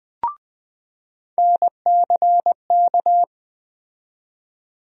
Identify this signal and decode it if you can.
Morse code, heard as a keyed tone, about 20 wpm, 700 Hz. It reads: NCK
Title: USA Call Sign Postfixes - 3 Letters 20wpm